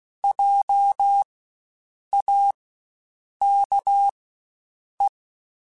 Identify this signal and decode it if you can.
Morse code, heard as a keyed tone, about 16 wpm, spaced as for 7 wpm, 800 Hz. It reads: JAKE
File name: Spaced Alphas / Spaced Alphas mp3s / Js sp @15